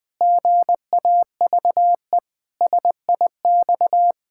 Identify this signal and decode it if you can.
Morse code, heard as a keyed tone, about 20 wpm, 700 Hz. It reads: GAVE SIX
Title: Pangram Challenge v2 - 20wpm